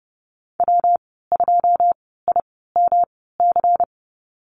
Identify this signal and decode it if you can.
Morse code, heard as a keyed tone, about 30 wpm, spaced as for 17 wpm, 700 Hz. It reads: W2IMC
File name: Lesson 19g - Letter w - Call Signs 30wpm